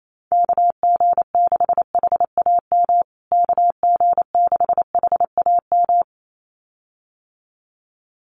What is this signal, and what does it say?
Morse code, heard as a keyed tone, about 28 wpm, 700 Hz. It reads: KG6HAM KG6HAM
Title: USA Call Signs Repeated 2x 28wpm